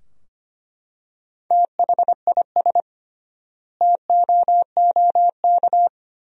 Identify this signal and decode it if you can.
Morse code, heard as a keyed tone, about 25 wpm, 700 Hz. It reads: THIS TOOK
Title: 3x Word-Spacing - Sentences from Top 300 Words - 25wpm